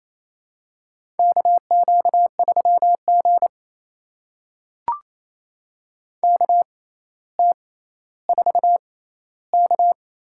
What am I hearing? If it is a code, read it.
Morse code, heard as a keyed tone, about 28 wpm, spaced as for 14 wpm, 700 Hz. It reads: KQ3G KT4K
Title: USA Call Signs - 4 Characters - 6x Character Spacing - 28wpm